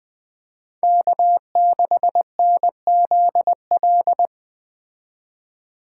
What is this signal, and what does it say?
Morse code, heard as a keyed tone, about 20 wpm, 700 Hz. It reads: K6NZL